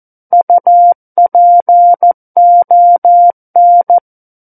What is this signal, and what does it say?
Morse code, heard as a keyed tone, about 14 wpm, 700 Hz. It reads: UPON
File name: Book One / 05-Chapter 5 - 14 WPM